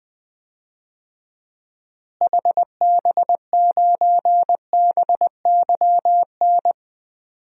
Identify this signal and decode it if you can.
Morse code, heard as a keyed tone, about 20 wpm, 700 Hz. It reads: HB9BYN
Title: DXCC-100 Call Signs - 6 Characters - Repeated 2x - 20wpm